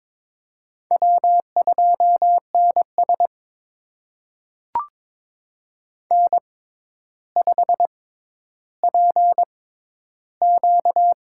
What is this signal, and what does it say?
Morse code, heard as a keyed tone, about 22 wpm, spaced as for 11 wpm, 700 Hz. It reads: W2NS N5PQ
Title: USA Call Signs - 4 Characters - 6x Character Spacing - 22wpm